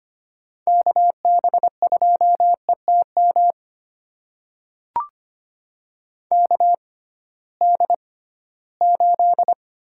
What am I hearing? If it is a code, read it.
Morse code, heard as a keyed tone, about 25 wpm, spaced as for 16 wpm, 700 Hz. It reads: KB2ETM KD8
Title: USA Call Signs - 6 Characters - 6x Character Spacing - 25wpm